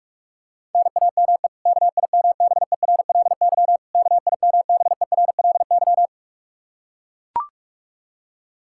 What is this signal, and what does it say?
Morse code, heard as a keyed tone, about 41 wpm, 700 Hz. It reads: NAME KIMBERLY KIMBERLY
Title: Practice QSO - Exchange 1 - Beginning - 45wpm